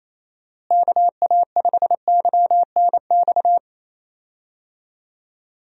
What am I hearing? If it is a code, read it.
Morse code, heard as a keyed tone, about 28 wpm, 700 Hz. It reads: KA5YNX
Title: USA Call Signs - 6 Characters - Repeated 2x - 28wpm